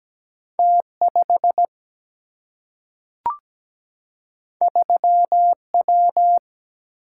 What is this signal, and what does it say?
Morse code, heard as a keyed tone, about 17 wpm, 700 Hz. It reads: T5 3W
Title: Single-Letter Single-Number 17wpm